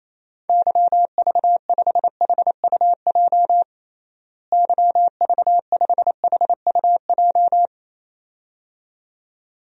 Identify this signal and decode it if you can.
Morse code, heard as a keyed tone, about 28 wpm, 700 Hz. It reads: YV5HUJ YV5HUJ
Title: DXCC-100 Call Signs - 6 Characters - Repeated 2x - 28wpm